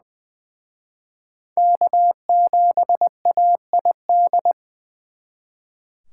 Morse code, heard as a keyed tone, about 25 wpm, 700 Hz. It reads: K7AID
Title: Speed Racing - USA 1x3 Call Signs 20wpm